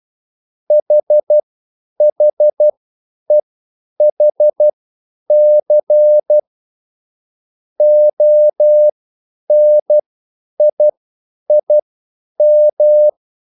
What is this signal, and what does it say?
Morse code, heard as a keyed tone, about 12 wpm, spaced as for 9 wpm, 600 Hz. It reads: HHEHC ONIIM
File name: QSL / CW lesson